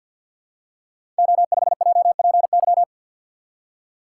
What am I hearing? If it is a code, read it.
Morse code, heard as a keyed tone, about 53 wpm, 700 Hz. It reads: O5JPY
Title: Code Groups - 5 Letter:Numbers - 50wpm